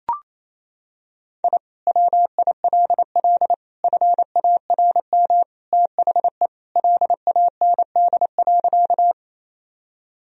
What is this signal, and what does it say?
Morse code, heard as a keyed tone, about 28 wpm, 700 Hz. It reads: I WILL FARM THE LAND.
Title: Sentences from Top 1000 Words - Review Part 2 - 28wpm